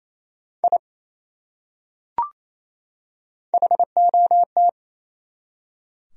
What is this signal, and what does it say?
Morse code, heard as a keyed tone, about 28 wpm, 700 Hz. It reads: I HOT
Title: Top 100 Words 28wpm